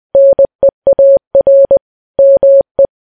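Morse code, heard as a keyed tone, about 20 wpm, 550 Hz. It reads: NEAR ME
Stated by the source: Morse code ringtone.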